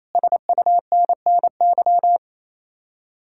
Morse code, heard as a keyed tone, about 32 wpm, 700 Hz. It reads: SUNNY
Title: Speed-Racing Common QSO Phrases Encore - 28wpm